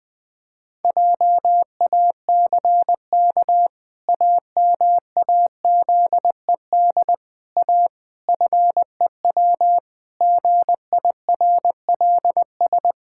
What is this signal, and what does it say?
Morse code, heard as a keyed tone, about 20 wpm, 700 Hz. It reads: JACK AMAZED A FEW GIRLS
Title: Pangram Challenge v2 - 20wpm